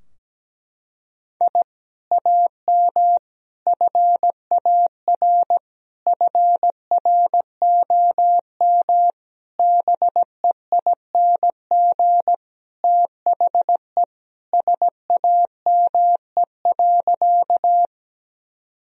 Morse code, heard as a keyed tone, about 17 wpm, 700 Hz. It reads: I AM FAR FROM BEING THE SAME.
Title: Sentences from Top 1000 Words - Review Part 3 - 17wpm